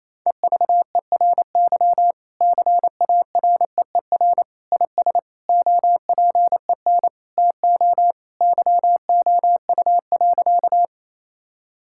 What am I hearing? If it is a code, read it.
Morse code, heard as a keyed tone, about 28 wpm, 700 Hz. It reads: EVERY CAREER IS OPEN TO YOU.
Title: Sentences from Top 1000 Words - Review Part 6 - 28wpm